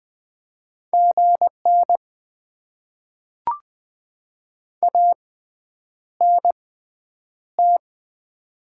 Morse code, heard as a keyed tone, about 20 wpm, spaced as for 9 wpm, 700 Hz. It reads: GN ANT
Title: Top 90 QSO Elements - 6x Character Spacing - 20wpm